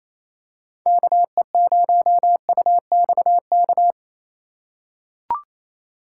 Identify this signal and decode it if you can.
Morse code, heard as a keyed tone, about 28 wpm, 700 Hz. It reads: KE0UXK
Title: USA Call Signs - 6 Characters - 3x Character Spacing - 28wpm